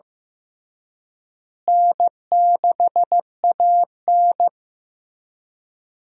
Morse code, heard as a keyed tone, about 15 wpm, 700 Hz. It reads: N6AN
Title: USA Call Signs - 4 Characters - Repeated 3x - 15wpm